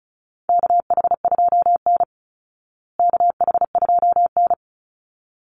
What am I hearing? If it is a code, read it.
Morse code, heard as a keyed tone, about 34 wpm, 700 Hz. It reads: KH2N KH2N
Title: Call Signs Repeated 3x - 35wpm